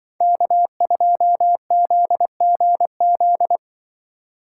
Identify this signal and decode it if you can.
Morse code, heard as a keyed tone, about 25 wpm, 700 Hz. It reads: K2ZGZ
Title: Speed Racing - USA Call Signs 20wpm